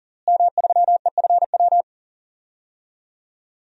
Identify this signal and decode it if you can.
Morse code, heard as a keyed tone, about 40 wpm, 690 Hz. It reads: M3EFW
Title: DXCC-100 Call Signs - 5 Characters - Repeated 2x - 40wpm